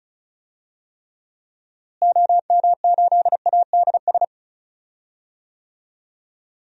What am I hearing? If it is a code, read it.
Morse code, heard as a keyed tone, about 36 wpm, 700 Hz. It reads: OM8ADS
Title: DXCC-100 Call Signs - 6 Characters - Repeated 3x - 35wpm